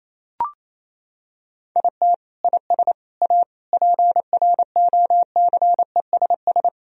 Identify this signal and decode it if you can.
Morse code, heard as a keyed tone, about 28 wpm, 700 Hz. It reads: IT IS A PROCESS
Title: Sentences from Top 1000 Words - Review Part 3 - 28wpm